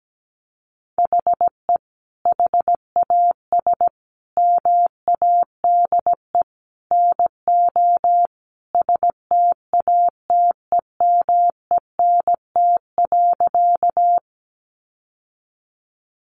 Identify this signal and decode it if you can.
Morse code, heard as a keyed tone, about 17 wpm, 700 Hz. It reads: HE HAS MADE NO STATEMENT.
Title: Sentences from Top 1200 Words 17wpm